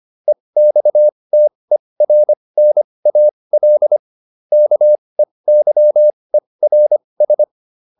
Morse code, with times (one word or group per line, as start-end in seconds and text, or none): 0.3-4.0 EXTERNAL
4.5-7.4 KEYERS